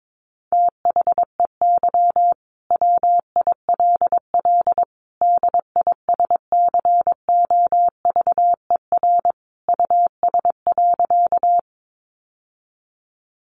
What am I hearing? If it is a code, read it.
Morse code, heard as a keyed tone, about 22 wpm, 700 Hz. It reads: THEY WILL DISCOVER US.
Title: Sentences from Top 1100 Words 22wpm